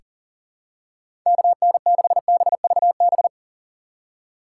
Morse code, heard as a keyed tone, about 37 wpm, 700 Hz. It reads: KN6BVB
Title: USA Call Signs - 4x Character Spacing - 40wpm